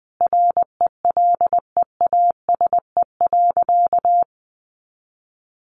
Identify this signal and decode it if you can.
Morse code, heard as a keyed tone, about 20 wpm, 700 Hz. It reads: RELEASE.
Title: A Sound Of Thunder 20wpm